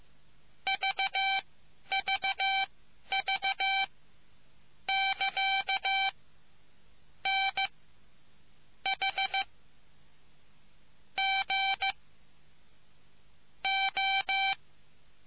Morse code, heard as a keyed tone, about 15 wpm, spaced as for 8 wpm, 780 Hz. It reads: VVV <KA> NH G O